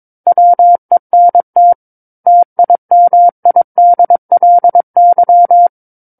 Morse code, heard as a keyed tone, about 22 wpm, 700 Hz. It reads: WENT TIMIDLY
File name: 06-Chapter-06 22 WPM